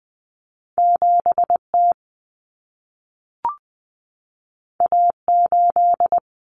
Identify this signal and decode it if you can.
Morse code, heard as a keyed tone, about 20 wpm, 700 Hz. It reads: 7T A8
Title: Common Cut Numbers - 2-Digit 20wpm